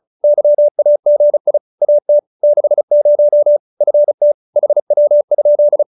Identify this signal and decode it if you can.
Morse code, heard as a keyed tone, about 36 wpm, 570 Hz. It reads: YAGI AT 60 FT HW?